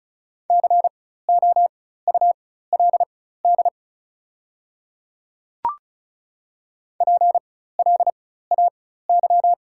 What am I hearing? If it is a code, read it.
Morse code, heard as a keyed tone, about 36 wpm, spaced as for 16 wpm, 700 Hz. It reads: COULD PLAY
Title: 4x Character Spacing - Top 200 Words - 35wpm